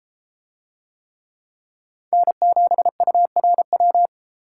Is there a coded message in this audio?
Yes: Morse code from 2.1 to 4.1 s, heard as a keyed tone, about 32 wpm, 700 Hz. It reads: N7URW